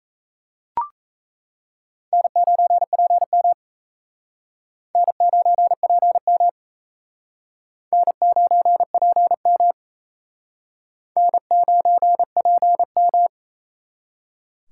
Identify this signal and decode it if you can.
Morse code, heard as a keyed tone, about 35 wpm, 700 Hz. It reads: N9PM N9PM N9PM N9PM